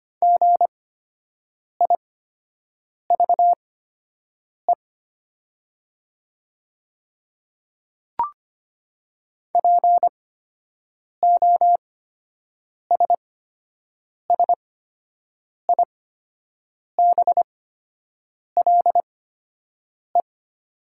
Morse code, heard as a keyed tone, about 25 wpm, spaced as for 7 wpm, 700 Hz. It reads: GIVE POSSIBLE
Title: 8x Character Spacing - Top 500 Words - 25wpm